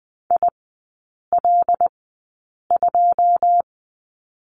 Morse code, heard as a keyed tone, about 20 wpm, spaced as for 8 wpm, 700 Hz. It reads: IL2
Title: 114 Lesson 19e - Letter w - Three Characters 20wpm